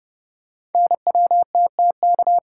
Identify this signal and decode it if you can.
Morse code, heard as a keyed tone, about 29 wpm, 700 Hz. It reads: NWTTK
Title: Code Groups - 5 Letters - 30wpm